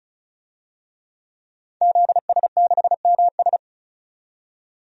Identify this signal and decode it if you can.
Morse code, heard as a keyed tone, about 36 wpm, 700 Hz. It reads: ZS6MS
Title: DXCC-100 Call Signs - 5 Characters - 35wpm